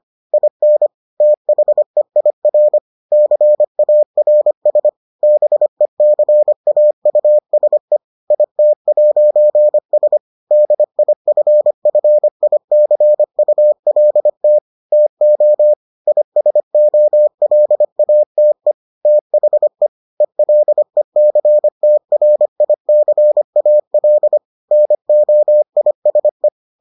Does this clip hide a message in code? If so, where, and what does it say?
Yes: Morse code from 0.3 to 26.5 s, heard as a keyed tone, about 25 wpm, 600 Hz. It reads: IN THEIR CARS BECAUSE IT'S DIFFICULT TO ISOLATE THE ELECTRICAL NOISE